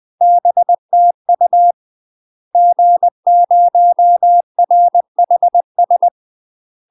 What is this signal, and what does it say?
Morse code, heard as a keyed tone, about 20 wpm, 700 Hz. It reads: BTU G0RHS
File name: UK MASTER QSOs TRAINING / 20WPM / UK MASTER QSOs 0001